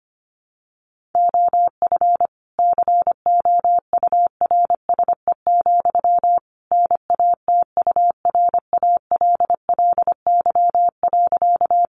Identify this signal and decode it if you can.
Morse code, heard as a keyed tone, about 25 wpm, 700 Hz. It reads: OF COURSE, NATURALLY.
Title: Sentences from Top 1300 Words 25wpm